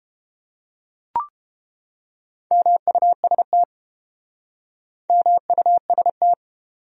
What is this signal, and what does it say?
Morse code, heard as a keyed tone, about 32 wpm, 700 Hz. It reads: MUST MUST